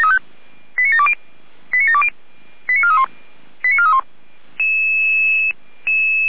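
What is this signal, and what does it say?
Five-tone selective call (ZVEI), heard as quick runs of five tones, tones of about 70 ms. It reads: 89720 89720 89431 89431
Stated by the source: Five-tone ZVEI selective call received on SDR.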